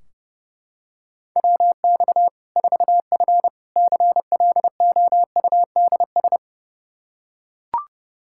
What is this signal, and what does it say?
Morse code, heard as a keyed tone, about 32 wpm, 700 Hz. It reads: WX 4F CLOUDS
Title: Practice QSO - Exchange 2 - Info - 30wpm